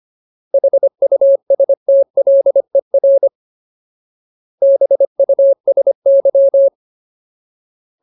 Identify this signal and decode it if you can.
Morse code, heard as a keyed tone, about 25 wpm, 550 Hz. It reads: HUSTLER BUSY
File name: Ultimate Word List / 25WPM Ultimate